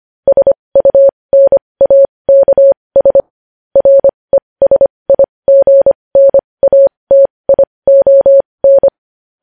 Morse code, heard as a keyed tone, about 25 wpm, spaced as for 20 wpm, 550 Hz. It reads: SUNAKS RESIGNATION
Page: news headlines in morse code at 20 wpm